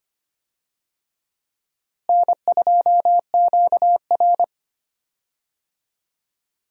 Morse code, heard as a keyed tone, about 25 wpm, 700 Hz. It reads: N2QR
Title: USA Call Signs - 4 Characters - Repeated 3x - 25wpm